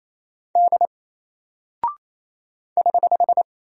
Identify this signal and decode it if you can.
Morse code, heard as a keyed tone, about 28 wpm, 700 Hz. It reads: D <HH>
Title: Single Character (Letter:Number:Punctuation:Prosign) Alt 28wpm